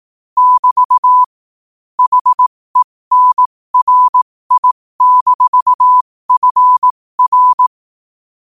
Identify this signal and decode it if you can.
Morse code, heard as a keyed tone, about 18 wpm, spaced as for 15 wpm, 1000 Hz. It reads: <BT> HENRI-FR
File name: qotdfiles / 15wpm